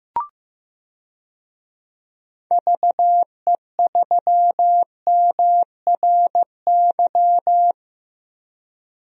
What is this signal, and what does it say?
Morse code, heard as a keyed tone, about 15 wpm, 700 Hz. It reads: VE3MRY
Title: DXCC-100 Call Signs - 6 Characters - Repeated 3x - 15wpm